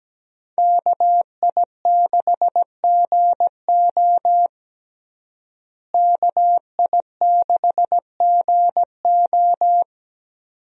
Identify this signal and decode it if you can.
Morse code, heard as a keyed tone, about 17 wpm, 700 Hz. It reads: KI6GO KI6GO
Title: USA Call Signs - 5 Characters - Repeated 2x - 17wpm